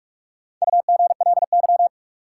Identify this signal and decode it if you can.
Morse code, heard as a keyed tone, about 46 wpm, 700 Hz. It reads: UGLY
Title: Four Letter Words - 45wpm